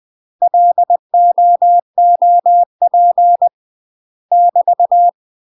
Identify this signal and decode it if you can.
Morse code, heard as a keyed tone, about 20 wpm, 700 Hz. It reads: LOOP <BT>
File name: UK MASTER QSOs TRAINING / 20WPM / UK MASTER QSOs 0071